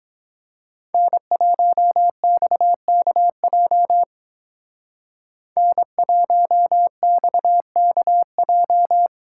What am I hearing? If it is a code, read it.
Morse code, heard as a keyed tone, about 20 wpm, 700 Hz. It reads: N1XKJ N1XKJ
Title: Speed Racing - USA 1x3 Call Signs 17wpm